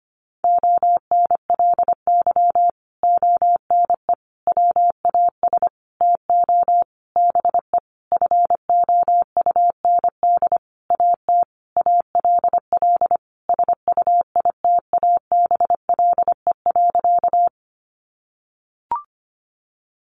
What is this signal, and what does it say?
Morse code, heard as a keyed tone, about 25 wpm, 700 Hz. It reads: ONLY ONE WAS TO BE FOUND AT ALL SUITABLE.
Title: Sentences from Top 1400 Words 25wpm